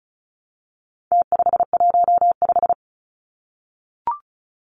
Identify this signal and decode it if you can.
Morse code, heard as a keyed tone, about 34 wpm, 700 Hz. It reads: T515
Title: Numbers - Contest Style - 4-Digit 35wpm